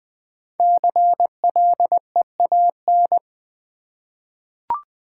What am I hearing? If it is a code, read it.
Morse code, heard as a keyed tone, about 20 wpm, 700 Hz. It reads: CLEAN
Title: 5 Letter Words - First 3 Letters Given - 20wpm